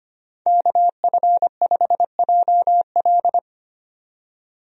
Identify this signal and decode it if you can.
Morse code, heard as a keyed tone, about 25 wpm, 700 Hz. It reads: KF5JL
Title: USA Call Signs - 5 Characters - 25wpm